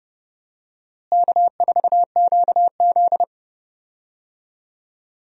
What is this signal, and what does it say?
Morse code, heard as a keyed tone, about 30 wpm, 700 Hz. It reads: K4QZ
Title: USA Call Signs - 4 Characters - Repeated 3x - 30wpm